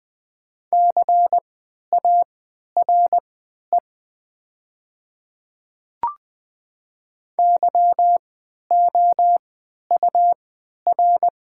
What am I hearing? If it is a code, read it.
Morse code, heard as a keyed tone, about 20 wpm, spaced as for 11 wpm, 700 Hz. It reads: CARE YOUR
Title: 3x Character Spacing - Top 300 Words - 20wpm